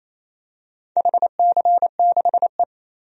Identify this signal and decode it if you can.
Morse code, heard as a keyed tone, about 28 wpm, 700 Hz. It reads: HC6E
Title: DXCC-100 Call Signs - 4 Characters - 28wpm